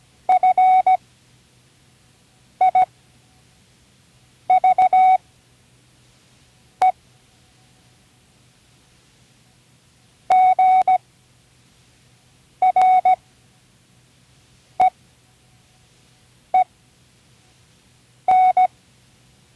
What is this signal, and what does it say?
Morse code, heard as a keyed tone, about 16 wpm, spaced as for 5 wpm, 740 Hz. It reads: FIVE GREEN
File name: CWResources / Learning Morse Code (from Tune in the World) / 29 words using all letters